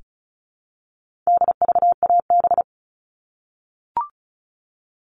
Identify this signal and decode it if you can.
Morse code, heard as a keyed tone, about 34 wpm, 700 Hz. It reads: DVAB
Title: All Cut Numbers - 4-Digit 35wpm